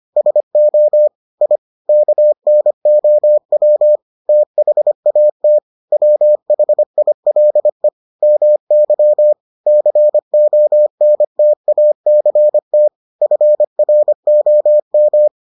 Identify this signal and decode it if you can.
Morse code, heard as a keyed tone, about 25 wpm, 600 Hz. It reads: SO I KNOW THAT WHILE MY CONTACT FROM